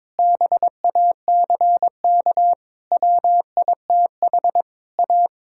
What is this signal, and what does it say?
Morse code, heard as a keyed tone, about 22 wpm, 700 Hz. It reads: BACK WITH A
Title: Sentences from Top 1000 Words - Review Part 5 - 22wpm